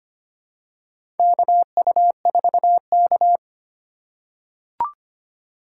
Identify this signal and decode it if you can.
Morse code, heard as a keyed tone, about 25 wpm, 700 Hz. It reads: KU4K
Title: USA Call Signs - 4 Characters - 3x Character Spacing - 25wpm